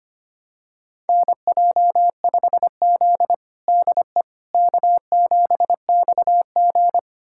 Morse code, heard as a keyed tone, about 25 wpm, 700 Hz. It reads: NJ5Z DE K7XG
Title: USA Callsign DE Callsign - 4 Characters - 25wpm